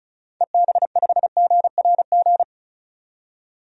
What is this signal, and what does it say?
Morse code, heard as a keyed tone, about 36 wpm, 690 Hz. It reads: EB5GRG
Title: DXCC-100 Call Signs - 6 Characters - 35wpm